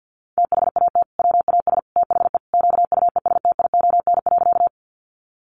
Morse code, heard as a keyed tone, about 50 wpm, 700 Hz. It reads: THAT WAS THE QUESTION.
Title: Sentences from Top 400 Words 50wpm